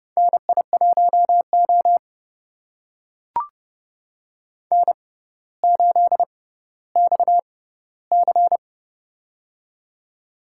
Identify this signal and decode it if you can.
Morse code, heard as a keyed tone, about 30 wpm, spaced as for 15 wpm, 700 Hz. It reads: NI1O N8XC
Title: USA Call Signs - 4 Characters - 6x Character Spacing - 30wpm